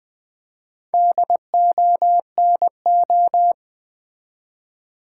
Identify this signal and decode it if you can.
Morse code, heard as a keyed tone, about 20 wpm, 700 Hz. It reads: DONO
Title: Build A Word - 1 Letter At A Time - 5-Letter Words - 20wpm